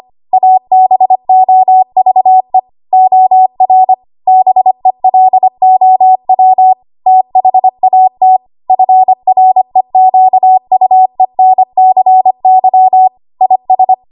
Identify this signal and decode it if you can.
Morse code, heard as a keyed tone, about 25 wpm, 750 Hz. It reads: ABOVE OR BELOW THAT FREQUENCY IS
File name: Morse / 25WPM